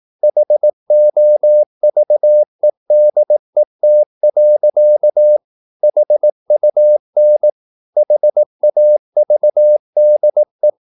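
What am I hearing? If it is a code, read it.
Morse code, heard as a keyed tone, about 18 wpm, 600 Hz. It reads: HOVEDET. HUN HAVDE
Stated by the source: Slow (QRS) Morse practice at 18 wpm, CW med Gnister.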